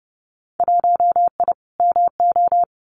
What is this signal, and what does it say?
Morse code, heard as a keyed tone, about 30 wpm, 700 Hz. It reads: 1I MO
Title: Winter Field Day Exchanges Encore 30wpm